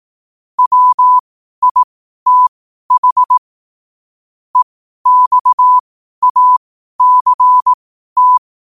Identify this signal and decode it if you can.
Morse code, heard as a keyed tone, about 18 wpm, spaced as for 12 wpm, 1000 Hz. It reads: WITH EXACT
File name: qotdfiles / 12wpm